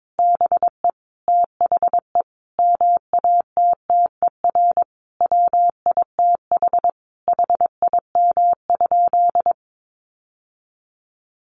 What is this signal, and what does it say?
Morse code, heard as a keyed tone, about 22 wpm, 700 Hz. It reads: BE THE MATTER WITH HIM?
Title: Sentences from Top 700 Words 22wpm